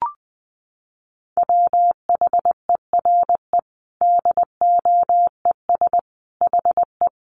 Morse code, heard as a keyed tone, about 20 wpm, 700 Hz. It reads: WHERE DOES HE
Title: Sentences from Top 200 Words 20wpm